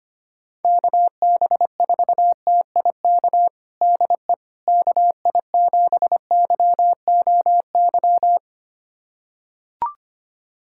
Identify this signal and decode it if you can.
Morse code, heard as a keyed tone, about 25 wpm, 700 Hz. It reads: KB4TIK DE KI7YOY
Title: USA Callsign DE Callsign - 6 Characters - 25wpm